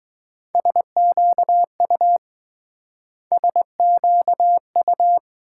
Morse code, heard as a keyed tone, about 22 wpm, 700 Hz. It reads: SQU SQU